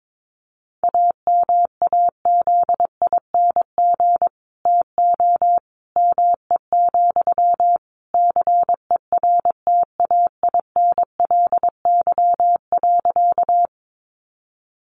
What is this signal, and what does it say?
Morse code, heard as a keyed tone, about 22 wpm, 700 Hz. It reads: AMAZING TO ME, CERTAINLY.
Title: Sentences from Top 1500 Words 22wpm